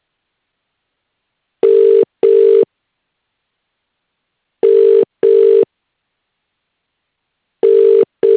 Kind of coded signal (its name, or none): telephone call progress tone